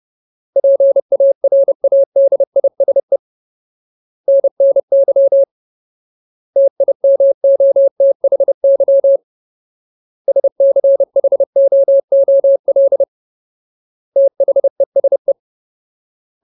Morse code, heard as a keyed tone, about 30 wpm, 550 Hz. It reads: PARADISE NNY TIMOTHY SCHOOL THESE
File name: Ultimate Word List / 30WPM Ultimate